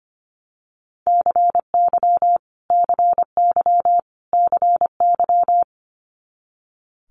Morse code, heard as a keyed tone, about 25 wpm, 700 Hz. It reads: CY CY CY
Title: Multi-letter Phonemes Repeated 3x 25wpm